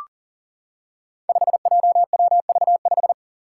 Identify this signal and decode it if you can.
Morse code, heard as a keyed tone, about 40 wpm, 690 Hz. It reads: 5JWV5
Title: Five Challenging Letter-Number Combinations 40wpm